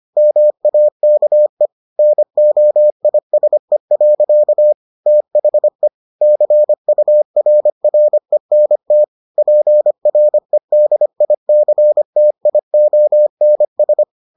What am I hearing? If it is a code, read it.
Morse code, heard as a keyed tone, about 25 wpm, 600 Hz. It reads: MAKE NOISE. THE CURRENT PREDICTIONS